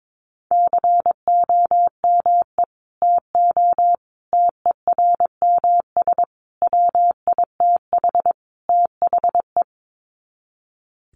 Morse code, heard as a keyed tone, about 22 wpm, 700 Hz. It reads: COME TO TERMS WITH THE
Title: Sets of 5 Words Encore2 22wpm